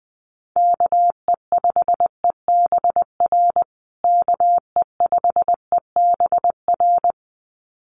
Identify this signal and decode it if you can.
Morse code, heard as a keyed tone, about 19 wpm, 700 Hz. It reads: KE5EBR KE5EBR
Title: USA Call Signs Repeated 2x 20wpm